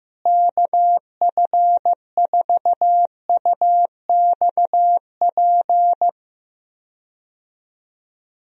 Morse code, heard as a keyed tone, about 15 wpm, 700 Hz. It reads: KF4UXP
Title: USA Call Signs - 6 Characters - Repeated 3x - 15wpm